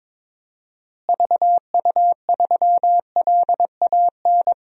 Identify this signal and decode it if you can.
Morse code, heard as a keyed tone, about 22 wpm, 700 Hz. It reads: VU3LAN
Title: DXCC-100 Call Signs - 6 Characters - Repeated 3x - 22wpm